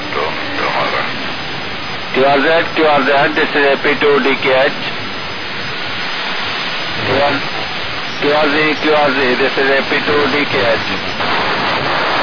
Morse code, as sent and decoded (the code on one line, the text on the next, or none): none